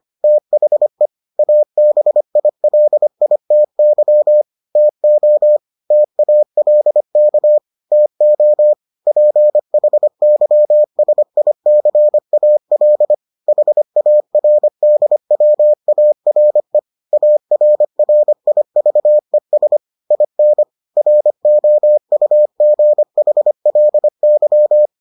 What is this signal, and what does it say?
Morse code, heard as a keyed tone, about 25 wpm, 600 Hz. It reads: THE ABILITY TO TALK TO PHYSICAL HARDWARE ARRIVES IN ROUGHLY